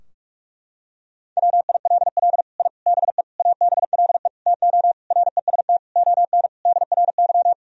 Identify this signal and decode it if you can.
Morse code, heard as a keyed tone, about 46 wpm, 700 Hz. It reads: WILL I BE ABLE TO REST ON DRY
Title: Sentences from Top 1000 Words - Review Part 3 - 45wpm